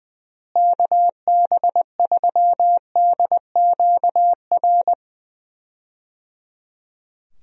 Morse code, heard as a keyed tone, about 20 wpm, 700 Hz. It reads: KB3DQR
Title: USA Call Signs - 6 Characters - Repeated 3x - 20wpm